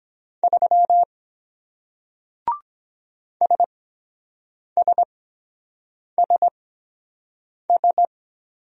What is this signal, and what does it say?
Morse code, heard as a keyed tone, about 23 wpm, spaced as for 7 wpm, 700 Hz. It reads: 3 SSSS